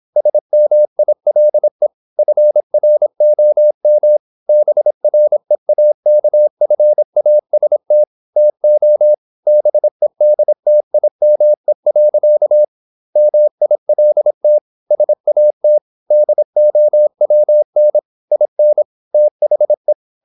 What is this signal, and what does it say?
Morse code, heard as a keyed tone, about 27 wpm, 600 Hz. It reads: SMILE FROM BREAKFAST TO BEDTIME. MILT SAT DOWN IN THE
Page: Never 26wpm | CW med Gnister